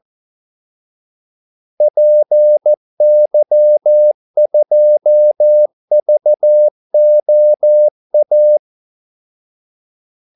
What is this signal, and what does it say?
Morse code, heard as a keyed tone, about 14 wpm, 600 Hz. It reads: PY2VOA
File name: Callsigns-14wpm